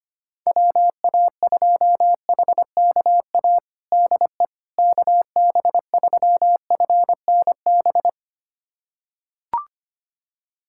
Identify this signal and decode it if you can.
Morse code, heard as a keyed tone, about 25 wpm, 700 Hz. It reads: WA2HKA DE KB3FNB
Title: USA Callsign DE Callsign - 6 Characters - 25wpm